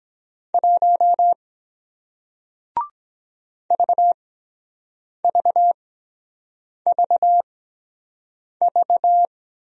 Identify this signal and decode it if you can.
Morse code, heard as a keyed tone, about 23 wpm, spaced as for 7 wpm, 700 Hz. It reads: 1 VVVV